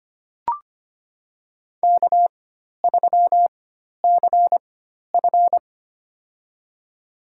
Morse code, heard as a keyed tone, about 25 wpm, spaced as for 12 wpm, 700 Hz. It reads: K3CF
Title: USA Call Signs - 4 Characters - 4x Character Spacing - 25wpm